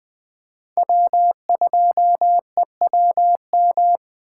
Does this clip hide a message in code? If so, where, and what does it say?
Yes: Morse code from 0.8 to 4.0 s, heard as a keyed tone, about 20 wpm, 700 Hz. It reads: W2EWM